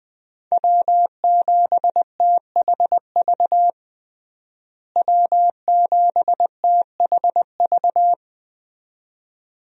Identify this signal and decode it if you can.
Morse code, heard as a keyed tone, about 20 wpm, 700 Hz. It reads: W7THV W7THV
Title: USA Call Signs - 5 Characters - Repeated 2x - 20wpm